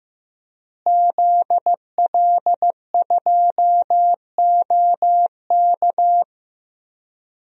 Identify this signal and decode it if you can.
Morse code, heard as a keyed tone, about 15 wpm, 700 Hz. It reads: ZL2OK
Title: DXCC-100 Call Signs - 5 Characters - Repeated 3x - 15wpm